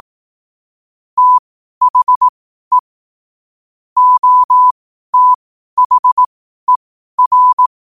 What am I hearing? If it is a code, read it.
Morse code, heard as a keyed tone, about 18 wpm, spaced as for 12 wpm, 1000 Hz. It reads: THE OTHER
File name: qotdfiles / 12wpm